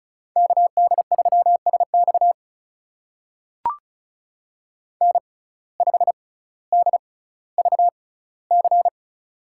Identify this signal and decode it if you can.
Morse code, heard as a keyed tone, about 34 wpm, spaced as for 18 wpm, 700 Hz. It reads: KD3SX N5DVC
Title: USA Call Signs - 5 Characters - 6x Character Spacing - 35wpm